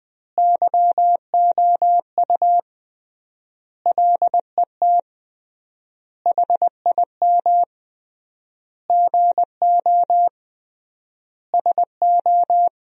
Morse code, heard as a keyed tone, about 20 wpm, 700 Hz. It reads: YOU LET HIM GO SO
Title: 3x Word-Spacing - Sentences from Top 300 Words - 20wpm